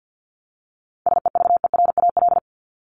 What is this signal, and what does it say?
Morse code, heard as a keyed tone, about 50 wpm, 700 Hz. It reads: SEVERAL
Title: Top 500 Words 50wpm